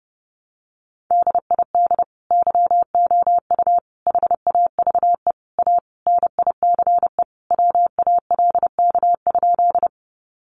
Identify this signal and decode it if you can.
Morse code, heard as a keyed tone, about 30 wpm, 700 Hz. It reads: DID YOU HAVE A NICE WALK?
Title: Sentences from Top 1100 Words 30wpm